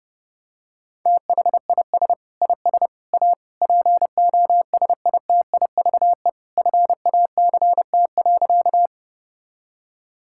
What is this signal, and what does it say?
Morse code, heard as a keyed tone, about 30 wpm, 700 Hz. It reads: THIS IS A POSITIVE FACT.
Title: Sentences from Top 1000 Words - Review Part 5 - 30wpm